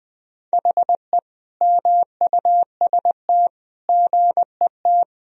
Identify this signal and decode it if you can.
Morse code, heard as a keyed tone, about 20 wpm, 700 Hz. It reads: HE MUST GET
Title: Sentences from Top 1000 Words - Review Part 1 - 20wpm